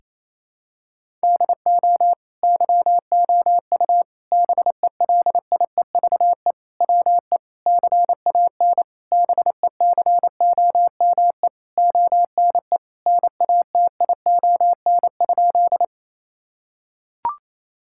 Morse code, heard as a keyed tone, about 28 wpm, 700 Hz. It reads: DO YOU BELIEVE WE CAN BECOME ONE NATION?
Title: Sentences from Top 1000 Words - Review Part 6 - 28wpm